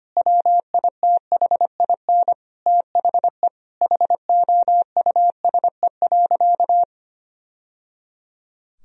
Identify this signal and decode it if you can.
Morse code, heard as a keyed tone, about 25 wpm, 700 Hz. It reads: WITHIN THE HOUSE.
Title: Sentences from Top 1000 Words - Review Part 4 - 25wpm